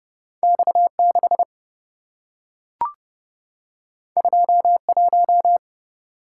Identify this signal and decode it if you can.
Morse code, heard as a keyed tone, about 27 wpm, 700 Hz. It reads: X6 21